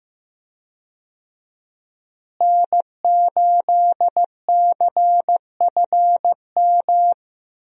Morse code, heard as a keyed tone, about 15 wpm, 700 Hz. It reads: N8CFM
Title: USA Call Signs - 5 Characters - Repeated 3x - 15wpm